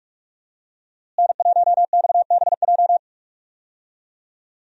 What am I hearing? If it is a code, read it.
Morse code, heard as a keyed tone, about 46 wpm, 700 Hz. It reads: N1XBJ
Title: USA Call Signs - 5 Characters - Repeated 2x - 45wpm